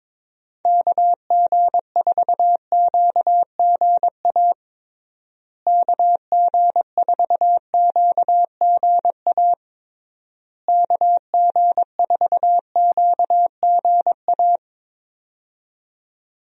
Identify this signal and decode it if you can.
Morse code, heard as a keyed tone, about 22 wpm, 700 Hz. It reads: KG4QGA KG4QGA KG4QGA
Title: USA Call Signs - 6 Characters - Repeated 3x - 22wpm